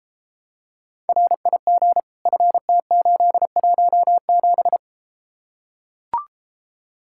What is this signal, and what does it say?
Morse code, heard as a keyed tone, about 33 wpm, 700 Hz. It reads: RIG FT817